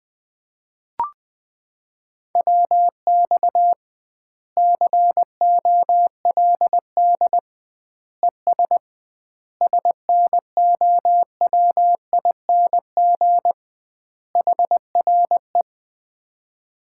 Morse code, heard as a keyed tone, about 20 wpm, 700 Hz. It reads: WX COLD ES SNOWING HRE
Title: Common QSO Phrases - 2x Word-Spacing - 20wpm